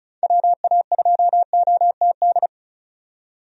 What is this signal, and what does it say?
Morse code, heard as a keyed tone, about 34 wpm, 700 Hz. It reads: WA2OTD
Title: USA Call Signs - 6 Characters - 35wpm